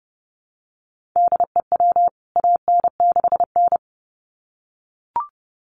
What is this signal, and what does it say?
Morse code, heard as a keyed tone, about 30 wpm, 700 Hz. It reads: DEW AN6N
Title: CWT Exchange 30wpm